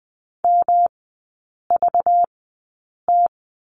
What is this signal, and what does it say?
Morse code, heard as a keyed tone, about 20 wpm, spaced as for 8 wpm, 700 Hz. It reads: MVT
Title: 191 Lesson 31e - Letter v - Three Characters 20wpm